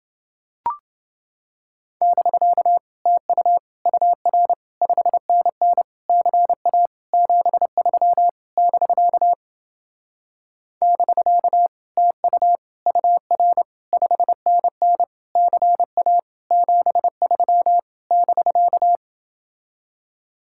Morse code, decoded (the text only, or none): <BK> TU UR 5NN CA 73 <BK> <BK> TU UR 5NN CA 73 <BK>